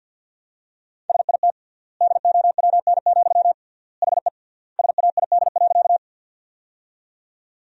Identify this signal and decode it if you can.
Morse code, heard as a keyed tone, about 50 wpm, 700 Hz. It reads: SIT DOWN, HE SAID.
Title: 3x Word-Spacing - Sentences from Top 500 Words - 50wpm